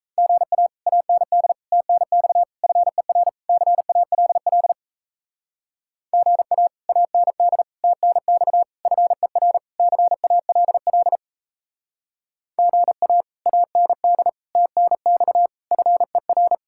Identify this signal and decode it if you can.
Morse code, heard as a keyed tone, about 34 wpm, 700 Hz. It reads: GA AND TNX FER CALL GA AND TNX FER CALL GA AND TNX FER
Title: Speed-Racing Common QSO Phrases Encore - 28wpm